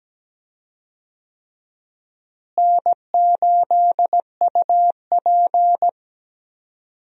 Morse code, heard as a keyed tone, about 17 wpm, 700 Hz. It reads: N8UP
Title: USA Call Signs - 4 Characters - Repeated 3x - 17wpm